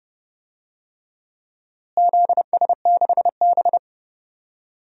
Morse code, heard as a keyed tone, about 30 wpm, 700 Hz. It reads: ZS6B
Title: DXCC-100 Call Signs - 4 Characters - Repeated 3x - 30wpm